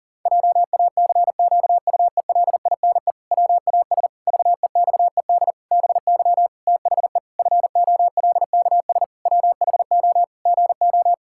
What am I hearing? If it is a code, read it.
Morse code, heard as a keyed tone, about 40 wpm, 700 Hz. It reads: JACQUELINE WAS VEXED BY THE FOLKS WHO GO
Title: Pangram Challenge v2 - 40wpm